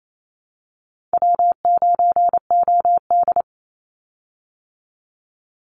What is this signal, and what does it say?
Morse code, heard as a keyed tone, about 28 wpm, 700 Hz. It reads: W9OD
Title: Call Signs Repeated 2x - 28wpm